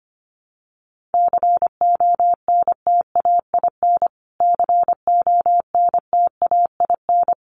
Morse code, heard as a keyed tone, about 25 wpm, 700 Hz. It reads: CONTAIN CONTAIN
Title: Top 500 Words Repeated 25wpm